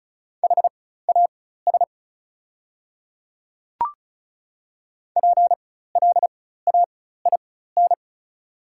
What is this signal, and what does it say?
Morse code, heard as a keyed tone, about 36 wpm, spaced as for 16 wpm, 700 Hz. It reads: HAS PLAIN
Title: 4x Character Spacing - Top 500 Words - 35wpm